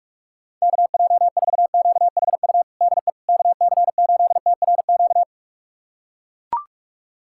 Morse code, heard as a keyed tone, about 42 wpm, 700 Hz. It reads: KJ4QHU DE KC8TRQ
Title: USA Callsign DE Callsign 45wpm